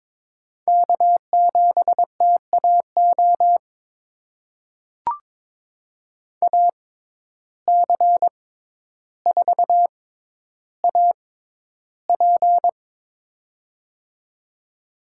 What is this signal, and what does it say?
Morse code, heard as a keyed tone, about 22 wpm, spaced as for 11 wpm, 700 Hz. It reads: K7TAO AC4AP
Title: USA Call Signs - 5 Characters - 6x Character Spacing - 22wpm